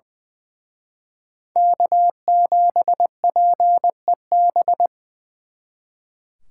Morse code, heard as a keyed tone, about 23 wpm, 700 Hz. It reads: K7PEB
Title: Speed Racing - USA Call Signs 20wpm